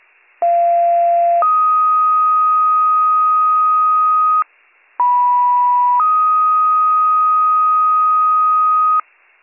Two-tone page, heard as a one-second tone, then a three-second tone, about 690 then 1250 Hz, about 980 then 1250 Hz. It is heard from 0.4 to 9.0 s.